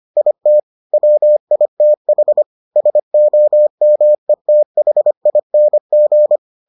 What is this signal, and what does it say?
Morse code, heard as a keyed tone, about 25 wpm, 600 Hz. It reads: IT WITH SOMETHING